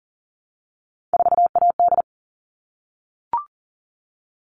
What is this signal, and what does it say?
Morse code, heard as a keyed tone, about 40 wpm, 700 Hz. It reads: 4AD
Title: All Cut Numbers - 3-Digit 40wpm